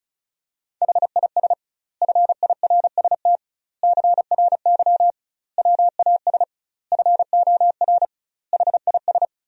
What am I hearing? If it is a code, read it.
Morse code, heard as a keyed tone, about 34 wpm, 700 Hz. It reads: HIS FIRST CRY WAS FOR HIS
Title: 2x Word-Spacing - Sentences from Top 500 Words - 35wpm